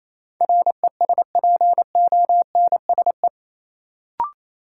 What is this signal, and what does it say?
Morse code, heard as a keyed tone, about 28 wpm, 700 Hz. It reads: RESPONSE
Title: Sentences from Top 1400 Words - New Word Drill Alt - 28wpm